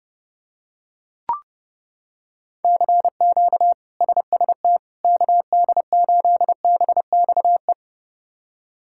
Morse code, heard as a keyed tone, about 30 wpm, 700 Hz. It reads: CQ SST KD8BXE
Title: K1USN SST QSO Practice - 30wpm